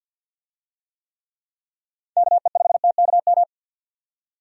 Morse code, heard as a keyed tone, about 52 wpm, 700 Hz. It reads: KE5TKM
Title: USA Call Signs - 6 Characters - 50wpm